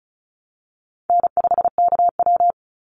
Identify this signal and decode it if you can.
Morse code, heard as a keyed tone, about 34 wpm, 700 Hz. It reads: N5KW
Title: Call Signs Repeated 3x - 35wpm